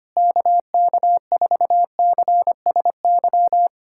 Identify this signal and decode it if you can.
Morse code, heard as a keyed tone, about 25 wpm, 700 Hz. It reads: KK4CSY
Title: USA Callsign DE Callsign - 6 Characters - 25wpm